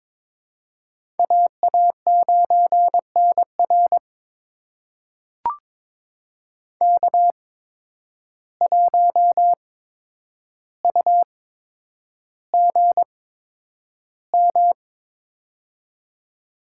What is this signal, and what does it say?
Morse code, heard as a keyed tone, about 23 wpm, spaced as for 10 wpm, 700 Hz. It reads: AA9NR K1UGM
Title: USA Call Signs - 5 Characters - 8x Character Spacing - 22wpm